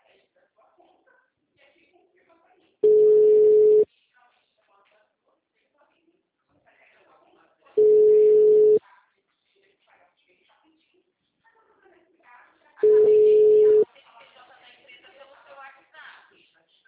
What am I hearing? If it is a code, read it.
Ringback tone (Europe, 425 Hz), with speech.
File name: out- -2010- - - .36654